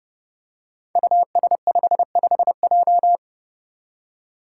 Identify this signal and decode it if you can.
Morse code, heard as a keyed tone, about 30 wpm, 700 Hz. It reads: US55J
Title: Five Challenging Letter-Number Combinations 30wpm